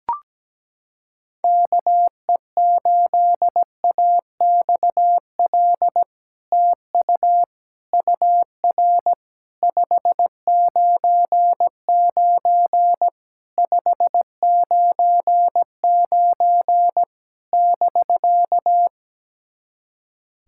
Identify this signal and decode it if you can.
Morse code, heard as a keyed tone, about 17 wpm, 700 Hz. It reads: KE8AXL TU UR 599 599 <BK>
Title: SOTA QSO Practice 17wpm